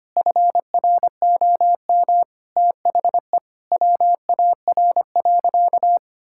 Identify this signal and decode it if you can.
Morse code, heard as a keyed tone, about 25 wpm, 700 Hz. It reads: FROM THE WAR.
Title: Sentences from Top 1000 Words - Review Part 6 - 25wpm